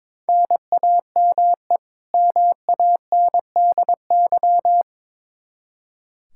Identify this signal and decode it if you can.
Morse code, heard as a keyed tone, about 22 wpm, 700 Hz. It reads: NAME MANDY
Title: Practice QSO - Exchange 1 - Beginning - 22wpm